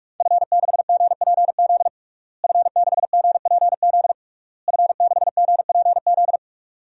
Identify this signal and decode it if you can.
Morse code, heard as a keyed tone, about 41 wpm, 700 Hz. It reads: F6GPZ F6GPZ F6GPZ
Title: DXCC-100 Call Signs - 5 Characters - Repeated 3x - 45wpm